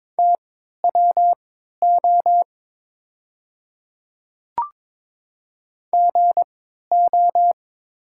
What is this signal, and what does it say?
Morse code, heard as a keyed tone, about 22 wpm, spaced as for 12 wpm, 690 Hz. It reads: TWO GO
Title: 3x Character Spacing - Top 100 Words - 22wpm